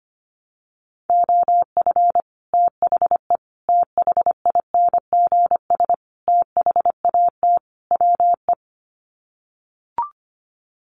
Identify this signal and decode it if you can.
Morse code, heard as a keyed tone, about 25 wpm, 700 Hz. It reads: OF THE THINGS THAT WE
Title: Sets of 5 Words Encore 25wpm